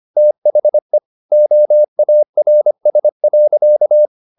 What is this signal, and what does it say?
Morse code, heard as a keyed tone, about 25 wpm, 600 Hz. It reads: THE OARS.
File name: the old man and the sea part 1 25wpm